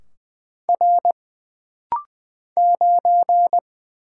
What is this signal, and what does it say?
Morse code, heard as a keyed tone, about 20 wpm, 700 Hz. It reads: R 9
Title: Single Letter-Number - Rapid-Fire 20wpm